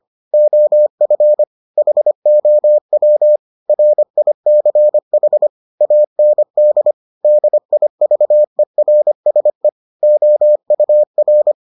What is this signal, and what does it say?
Morse code, heard as a keyed tone, about 25 wpm, 600 Hz. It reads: OF HOW RICH AND DIVERSE OUR